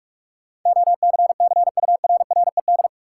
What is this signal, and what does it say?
Morse code, heard as a keyed tone, about 44 wpm, 700 Hz. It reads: OCCURRED
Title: Sets of 5 Words Encore - New Word Drill - 45wpm